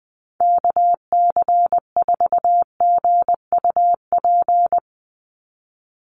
Morse code, heard as a keyed tone, about 20 wpm, 700 Hz. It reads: KC4GUP
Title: USA Call Signs 20wpm